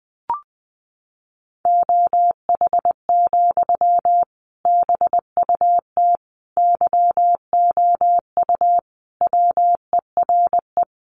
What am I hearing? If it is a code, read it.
Morse code, heard as a keyed tone, about 20 wpm, 700 Hz. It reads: OH, BUT YOU WERE
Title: Sentences from Top 500 Words 20wpm